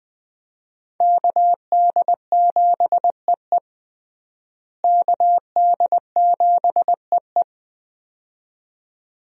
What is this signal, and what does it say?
Morse code, heard as a keyed tone, about 20 wpm, 700 Hz. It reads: KD7EE KD7EE
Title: USA Call Signs - 5 Characters - Repeated 2x - 20wpm